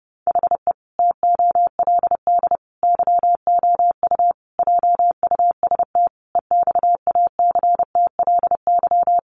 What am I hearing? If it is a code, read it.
Morse code, heard as a keyed tone, about 30 wpm, 700 Hz. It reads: HE TOLD YOU JUST EXACTLY
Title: Sentences from Top 900 Words 30wpm